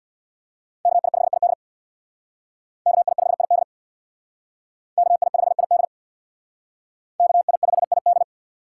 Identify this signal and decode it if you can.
Morse code, heard as a keyed tone, about 61 wpm, 700 Hz. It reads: KI5ID KI5ID KI5ID KI5ID
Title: Speed Racing - USA 2x2 Call Signs 50wpm